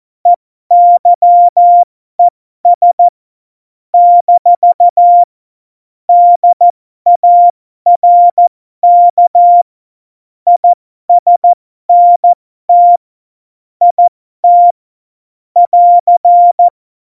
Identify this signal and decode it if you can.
Morse code, heard as a keyed tone, about 14 wpm, spaced as for 12 wpm, 700 Hz. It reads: EYES - DARK ISNT IT <AR>
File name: Morsekurs / Witze